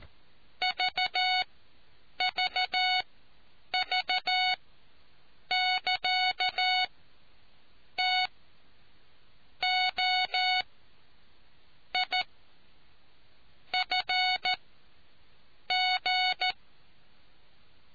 Morse code, heard as a keyed tone, about 14 wpm, spaced as for 6 wpm, 740 Hz. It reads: VVV <KA> TOIFG